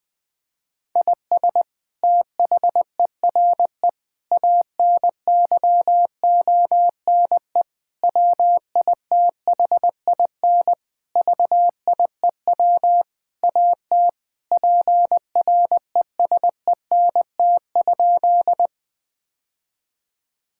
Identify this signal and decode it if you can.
Morse code, heard as a keyed tone, about 20 wpm, 700 Hz. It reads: IS THERE ANYONE WITHIN VIEW AT PRESENT?
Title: Sentences from Top 1000 Words - Review Part 6 - 20wpm